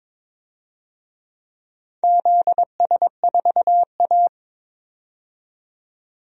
Morse code, heard as a keyed tone, about 22 wpm, 700 Hz. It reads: ZS4A
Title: DXCC-100 Call Signs - 4 Characters - 22wpm